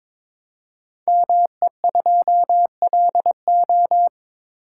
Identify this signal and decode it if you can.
Morse code, heard as a keyed tone, about 22 wpm, 700 Hz. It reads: ME2LO
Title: Code Groups - 5 Letter:Numbers - 22wpm